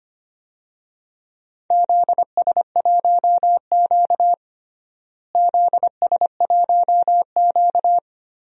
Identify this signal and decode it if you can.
Morse code, heard as a keyed tone, about 25 wpm, 700 Hz. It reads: ZS1Q ZS1Q
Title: DXCC-100 Call Signs - 4 Characters - Repeated 3x - 25wpm